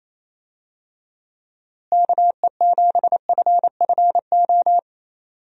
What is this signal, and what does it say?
Morse code, heard as a keyed tone, about 28 wpm, 700 Hz. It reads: KE7FFO
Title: USA Call Signs - 6 Characters - Repeated 3x - 28wpm